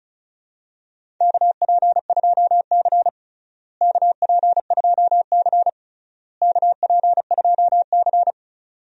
Morse code, heard as a keyed tone, about 34 wpm, 700 Hz. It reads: KP2C KP2C KP2C
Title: DXCC-100 Call Signs - 4 Characters - Repeated 3x - 35wpm